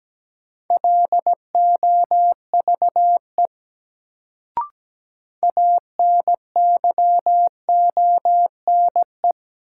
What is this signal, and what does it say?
Morse code, heard as a keyed tone, about 17 wpm, 700 Hz. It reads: LOVE ANYONE
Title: Sets of 5 Words Encore - New Word Drill - 17wpm